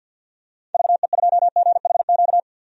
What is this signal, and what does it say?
Morse code, heard as a keyed tone, about 46 wpm, 700 Hz. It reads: VE2GHQ
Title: DXCC-100 Call Signs - 6 Characters - 50wpm